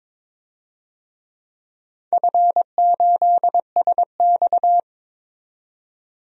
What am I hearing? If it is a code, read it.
Morse code, heard as a keyed tone, about 22 wpm, 700 Hz. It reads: F8SX
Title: DXCC-100 Call Signs - 4 Characters - 22wpm